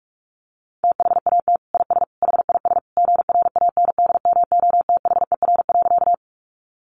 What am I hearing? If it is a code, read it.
Morse code, heard as a keyed tone, about 44 wpm, 700 Hz. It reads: THAT IS HIS GRANDMOTHER.
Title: Sentences from Top 1200 Words 45wpm